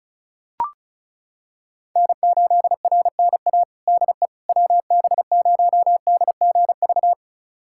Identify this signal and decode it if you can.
Morse code, heard as a keyed tone, about 34 wpm, 700 Hz. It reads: N8RNA DE WB0DGV
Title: USA Callsign DE Callsign 35wpm